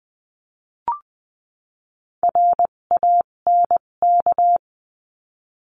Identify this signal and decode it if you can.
Morse code, heard as a keyed tone, about 20 wpm, spaced as for 17 wpm, 700 Hz. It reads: RANK
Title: 257 Lesson 42b - Words 14-20wpm